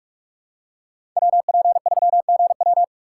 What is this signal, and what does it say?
Morse code, heard as a keyed tone, about 41 wpm, 700 Hz. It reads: WP3GW
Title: USA Call Signs - 5 Characters - 6x Character Spacing - 45wpm